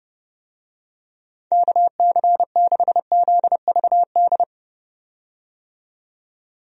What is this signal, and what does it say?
Morse code, heard as a keyed tone, about 32 wpm, 700 Hz. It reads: KC6ZVD
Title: USA Call Signs - 6 Characters - 30wpm